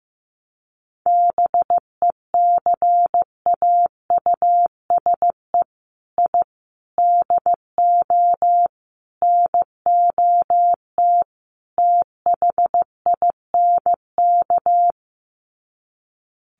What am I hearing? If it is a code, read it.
Morse code, heard as a keyed tone, about 15 wpm, 700 Hz. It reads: BECAUSE I DO NOT THINK
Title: Sets of 5 Words Encore 15wpm